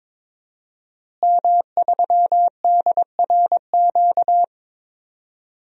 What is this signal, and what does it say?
Morse code, heard as a keyed tone, about 22 wpm, 700 Hz. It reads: M3DRQ
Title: DXCC-100 Call Signs - 5 Characters - Repeated 2x - 22wpm